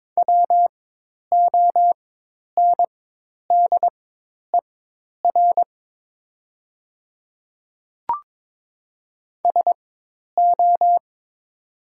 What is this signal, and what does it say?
Morse code, heard as a keyed tone, about 22 wpm, spaced as for 10 wpm, 690 Hz. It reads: WONDER SO
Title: 4x Character Spacing - Top 500 Words - 22wpm